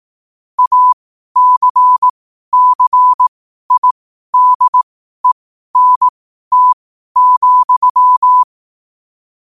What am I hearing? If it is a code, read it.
Morse code, heard as a keyed tone, about 18 wpm, spaced as for 12 wpm, 1000 Hz. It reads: ACCIDENT,
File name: qotdfiles / 12wpm